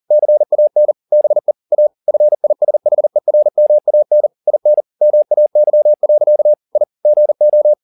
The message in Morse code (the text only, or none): CAN BE A FISHERMAN IN MAY. I GO